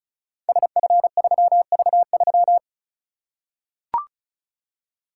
Morse code, heard as a keyed tone, about 33 wpm, 700 Hz. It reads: SF3V3
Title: Code Groups - 5 Letter:Numbers - 35wpm